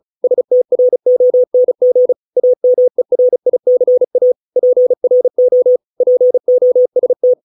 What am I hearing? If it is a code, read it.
Morse code, heard as a keyed tone, about 34 wpm, 500 Hz. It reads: STRONG AMERICA PRO POST